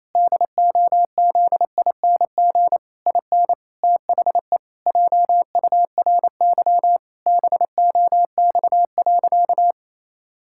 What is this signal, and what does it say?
Morse code, heard as a keyed tone, about 28 wpm, 700 Hz. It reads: DOZING IN THE JURY BOX.
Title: Pangram Challenge v2 - 28wpm